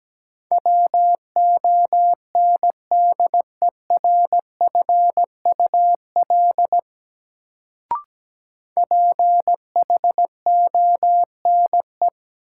Morse code, heard as a keyed tone, about 17 wpm, 700 Hz. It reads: WONDERFUL PHONE
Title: Sentences from Top 1100 Words - New Word Drill - 17wpm